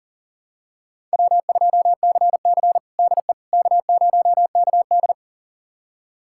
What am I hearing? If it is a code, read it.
Morse code, heard as a keyed tone, about 40 wpm, 700 Hz. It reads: W2CC DE K0KD
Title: USA Callsign DE Callsign - 4 Characters - 40wpm